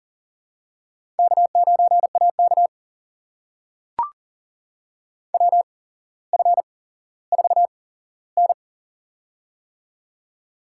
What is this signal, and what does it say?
Morse code, heard as a keyed tone, about 42 wpm, spaced as for 17 wpm, 700 Hz. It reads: K9AK WF4N
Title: USA Call Signs - 4 Characters - 8x Character Spacing - 40wpm